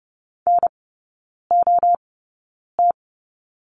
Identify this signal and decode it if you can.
Morse code, heard as a keyed tone, about 30 wpm, spaced as for 9 wpm, 700 Hz. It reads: NOT
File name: course / 30:10wpm / 087 Lesson 15e - Number 5 - Three Characters 30wpm